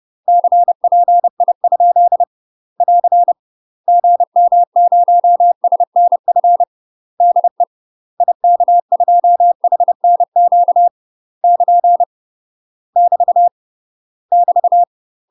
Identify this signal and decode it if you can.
Morse code, heard as a keyed tone, about 30 wpm, 700 Hz. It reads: CPI? <AR> GM0SNF DE IK2HNQ <KN> <BT> <BT>
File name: UK MASTER QSOs TRAINING / 30WPM / UK MASTER QSOs 0015